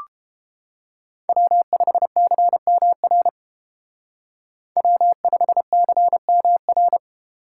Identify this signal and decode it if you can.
Morse code, heard as a keyed tone, about 31 wpm, 700 Hz. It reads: W5CMR W5CMR